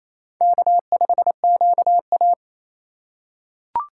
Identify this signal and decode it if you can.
Morse code, heard as a keyed tone, about 28 wpm, 700 Hz. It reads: K5QA
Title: USA Call Signs - 8x Character Spacing - 28wpm